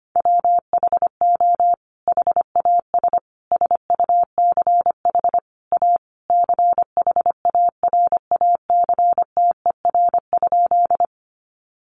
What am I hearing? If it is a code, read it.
Morse code, heard as a keyed tone, about 25 wpm, 700 Hz. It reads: WHO HAS SUCH A CHARACTER?
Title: Sentences from Top 1200 Words 25wpm